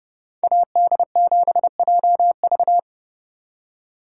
Morse code, heard as a keyed tone, about 29 wpm, 700 Hz. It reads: AD7JV